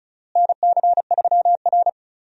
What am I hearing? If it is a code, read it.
Morse code, heard as a keyed tone, about 36 wpm, 700 Hz. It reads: NC3R
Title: USA Call Signs - 4 Characters - Repeated 2x - 35wpm